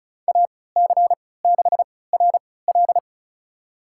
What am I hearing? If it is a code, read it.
Morse code, heard as a keyed tone, about 36 wpm, spaced as for 20 wpm, 690 Hz. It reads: AC6RL
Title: USA Call Signs - 5 Characters - 3x Character Spacing - 35wpm